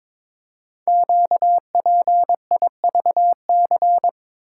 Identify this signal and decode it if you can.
Morse code, heard as a keyed tone, about 22 wpm, 700 Hz. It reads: QPIVC
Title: Code Groups - 5 Letters - 22wpm